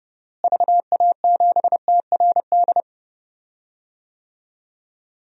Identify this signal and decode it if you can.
Morse code, heard as a keyed tone, about 30 wpm, 700 Hz. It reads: VA7TRD
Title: DXCC-100 Call Signs - 6 Characters - Repeated 2x - 30wpm